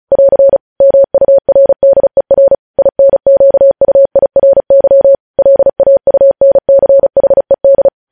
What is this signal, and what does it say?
Morse code, heard as a keyed tone, about 36 wpm, 550 Hz. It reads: <AR> MURDER INQUIRY LAUNCHED
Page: news headlines in morse code at 35 wpm